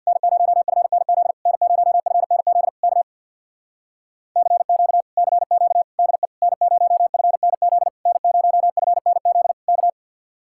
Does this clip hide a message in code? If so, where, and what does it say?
Yes: Morse code from 0.1 to 9.9 s, heard as a keyed tone, about 64 wpm, 700 Hz. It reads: N0FNZ N0FNZ K CQ CQ DE N0FNZ N0FNZ K